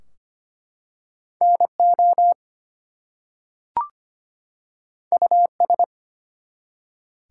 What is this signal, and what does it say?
Morse code, heard as a keyed tone, about 25 wpm, 700 Hz. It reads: NO US
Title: Two Letter Words - 25wpm